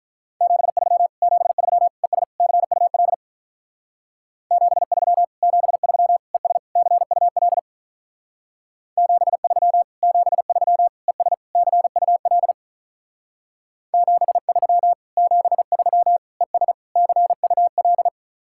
Morse code, heard as a keyed tone, about 44 wpm, 700 Hz. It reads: 73 73 ES CUL 73 73 ES CUL 73 73 ES CUL 73 73 ES CUL
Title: Speed-Racing Common QSO Phrases - 35wpm